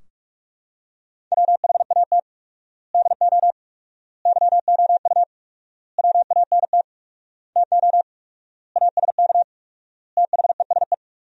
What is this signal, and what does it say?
Morse code, heard as a keyed tone, about 46 wpm, 700 Hz. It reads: WHAT DO YOU WANT TO ASK THESE
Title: 4x Word-Spacing - Sentences from Top 200 Words - 45wpm